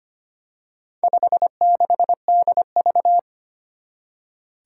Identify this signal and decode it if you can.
Morse code, heard as a keyed tone, about 25 wpm, 700 Hz. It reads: 56DV
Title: Four Challenging Letter-Number Combinations 25wpm